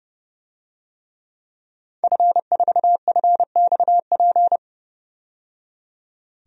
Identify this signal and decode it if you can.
Morse code, heard as a keyed tone, about 31 wpm, 700 Hz. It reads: F4FXP
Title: DXCC-100 Call Signs - 5 Characters - 30wpm